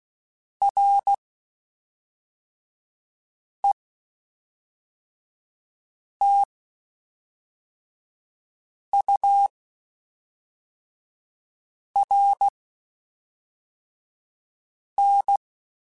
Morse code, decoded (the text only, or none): RETURN